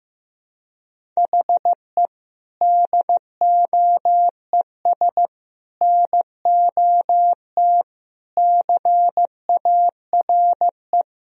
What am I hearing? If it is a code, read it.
Morse code, heard as a keyed tone, about 15 wpm, 700 Hz. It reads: HE DOES NOT CARE
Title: Sentences from Top 1000 Words - Review Part 1 - 15wpm